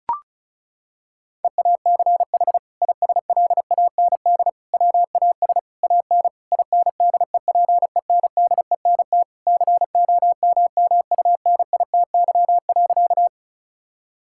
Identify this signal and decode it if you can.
Morse code, heard as a keyed tone, about 36 wpm, 700 Hz. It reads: EACH ISLAND WAS AN INDEPENDENT COMMUNITY.
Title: Sentences from Top 1000 Words - Review Part 6 - 35wpm